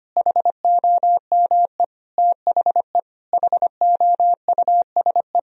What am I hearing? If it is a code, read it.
Morse code, heard as a keyed tone, about 25 wpm, 700 Hz. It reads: HOME THE HOUSE
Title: Sentences from Top 500 Words and Abbreviations 25wpm